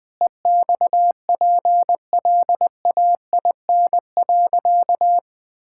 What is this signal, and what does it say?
Morse code, heard as a keyed tone, about 20 wpm, 700 Hz. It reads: EXPLAIN.
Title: Sentences with 5-Word Phrases - Spoken Beginning - 20wpm